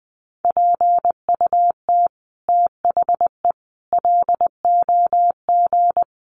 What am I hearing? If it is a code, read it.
Morse code, heard as a keyed tone, about 20 wpm, 700 Hz. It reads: PUT THE LOG
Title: Sentences from Top 1400 Words 20wpm